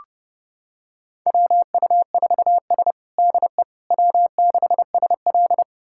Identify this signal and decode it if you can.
Morse code, heard as a keyed tone, about 30 wpm, 700 Hz. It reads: WU4S DE W6SL
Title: USA Callsign DE Callsign - 4 Characters - 30wpm